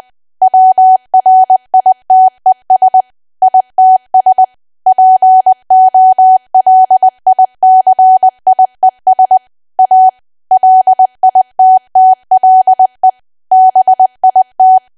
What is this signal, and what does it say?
Morse code, heard as a keyed tone, about 20 wpm, 750 Hz. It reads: WRITES ITS POLICIES A LITTLE BIT
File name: cwa / W1AW-Level3 / 20WPM